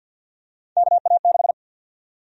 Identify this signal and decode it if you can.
Morse code, heard as a keyed tone, about 52 wpm, 700 Hz. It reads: KA6
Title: USA Call Sign Prefixes - 50wpm